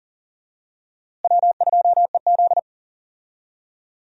Morse code, heard as a keyed tone, about 40 wpm, 700 Hz. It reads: W2EZ
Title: USA Call Signs - 4 Characters - Repeated 3x - 40wpm